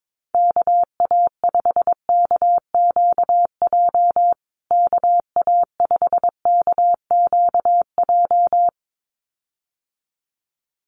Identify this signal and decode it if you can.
Morse code, heard as a keyed tone, about 22 wpm, 700 Hz. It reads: KA5KQJ KA5KQJ
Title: USA Call Signs Repeated 2x 22wpm